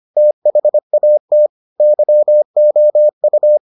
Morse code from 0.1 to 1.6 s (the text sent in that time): THAT